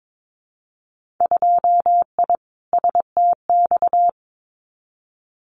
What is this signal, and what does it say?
Morse code, heard as a keyed tone, about 22 wpm, 700 Hz. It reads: 2I STX
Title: Winter Field Day Exchanges 22wpm